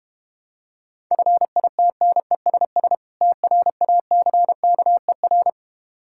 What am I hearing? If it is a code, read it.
Morse code, heard as a keyed tone, about 32 wpm, 700 Hz. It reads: FITNESS TRACKER